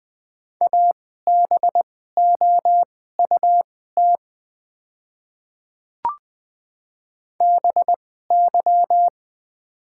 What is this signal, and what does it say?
Morse code, heard as a keyed tone, about 20 wpm, spaced as for 14 wpm, 700 Hz. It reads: ABOUT BY
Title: 2x Character Spacing - Top 200 Words - 20wpm